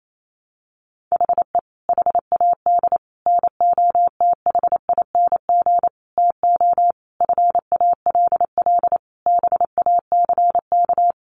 Morse code, heard as a keyed tone, about 28 wpm, 700 Hz. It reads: HE HAD NOTHING TO FALL BACK
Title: Sentences from Top 1100 Words 28wpm